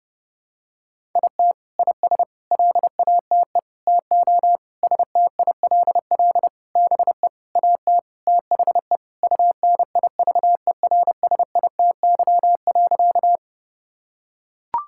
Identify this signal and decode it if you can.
Morse code, heard as a keyed tone, about 30 wpm, 700 Hz. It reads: IT IS LATE TO STILL BE AT THE UNIVERSITY.
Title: Sentences from Top 1000 Words - Review Part 5 - 30wpm